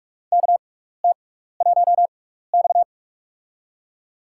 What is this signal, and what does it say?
Morse code, heard as a keyed tone, about 46 wpm, spaced as for 16 wpm, 700 Hz. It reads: KT1X
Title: USA Call Signs - 4 Characters - 6x Character Spacing - 45wpm